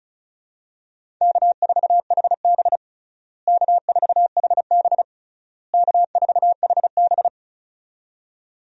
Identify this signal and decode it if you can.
Morse code, heard as a keyed tone, about 36 wpm, 700 Hz. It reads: K4HB K4HB K4HB
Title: USA Call Signs - 4 Characters - Repeated 3x - 35wpm